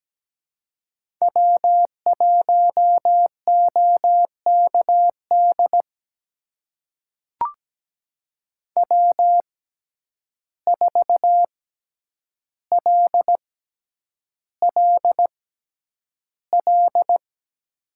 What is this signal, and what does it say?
Morse code, heard as a keyed tone, about 17 wpm, spaced as for 9 wpm, 700 Hz. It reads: W1OKD W4LLL
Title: USA Call Signs - 5 Characters - 6x Character Spacing - 17wpm